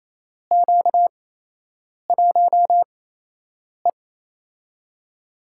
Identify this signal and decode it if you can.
Morse code, heard as a keyed tone, about 27 wpm, spaced as for 8 wpm, 700 Hz. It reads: Q1E
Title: USA Call Signs - 4 Characters - 8x Character Spacing - 28wpm